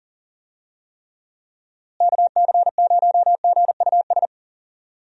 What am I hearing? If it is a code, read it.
Morse code, heard as a keyed tone, about 40 wpm, 700 Hz. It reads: KC0GUS
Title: USA Call Signs - 6 Characters - Repeated 2x - 40wpm